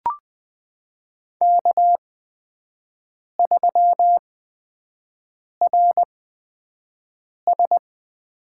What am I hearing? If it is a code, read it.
Morse code, heard as a keyed tone, about 20 wpm, spaced as for 5 wpm, 700 Hz. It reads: K3RS
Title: USA Call Signs - 5 Characters - 8x Character Spacing - 20wpm